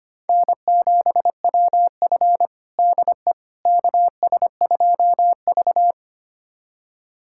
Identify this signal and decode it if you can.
Morse code, heard as a keyed tone, about 25 wpm, 700 Hz. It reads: N7WF DE KS2V
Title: USA Callsign DE Callsign - 4 Characters - 25wpm